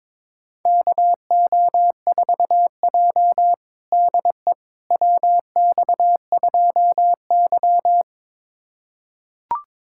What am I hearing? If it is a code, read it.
Morse code, heard as a keyed tone, about 21 wpm, 700 Hz. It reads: KO4J DE WX2Y
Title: USA Callsign DE Callsign - 4 Characters - 22wpm